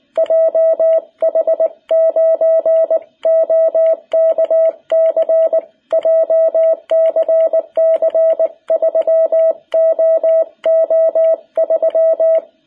Morse code, heard as a keyed tone, about 19 wpm, 660 Hz. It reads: JH9OKC JCC 3OO3
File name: jh8cbh / 2021cw